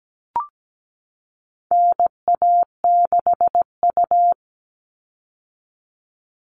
Morse code, heard as a keyed tone, about 17 wpm, 700 Hz. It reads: NA6U
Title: All Cut Numbers - 4-Digit 17wpm